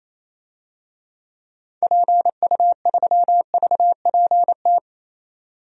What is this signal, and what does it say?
Morse code, heard as a keyed tone, about 27 wpm, 700 Hz. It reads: PU3VPT
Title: DXCC-100 Call Signs - 6 Characters - 28wpm